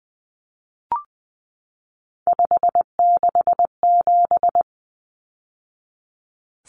Morse code, heard as a keyed tone, about 20 wpm, 700 Hz. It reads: 567
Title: Numbers - 3-Digit 20wpm